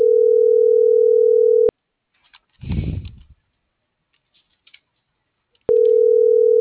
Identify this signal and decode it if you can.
Ringback tone (North America), 440 + 480 Hz.